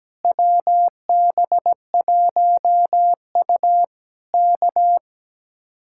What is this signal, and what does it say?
Morse code, heard as a keyed tone, about 17 wpm, 700 Hz. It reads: WB1U K
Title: Speed-Racing Common QSO Phrases - 17wpm